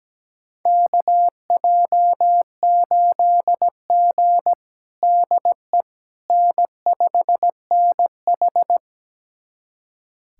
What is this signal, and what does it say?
Morse code, heard as a keyed tone, about 17 wpm, 700 Hz. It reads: KJ8G DE N5NH
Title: USA Callsign DE Callsign - 4 Characters - 17wpm